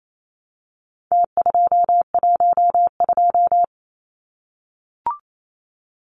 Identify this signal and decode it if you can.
Morse code, heard as a keyed tone, about 28 wpm, 700 Hz. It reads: T212
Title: Numbers - Contest Style - 4-Digit 28wpm